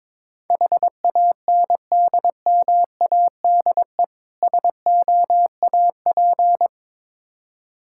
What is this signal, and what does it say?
Morse code, heard as a keyed tone, about 22 wpm, 700 Hz. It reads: HANDMADE SOAP
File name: Difficult Words with Hints - 22wpm